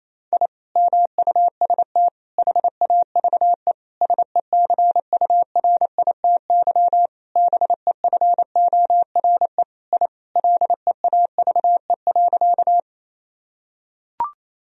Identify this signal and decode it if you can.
Morse code, heard as a keyed tone, about 27 wpm, 700 Hz. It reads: I MUST HAVE SECURITY BEFORE I LEAVE.
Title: Sentences from Top 1000 Words - Review Part 4 - 28wpm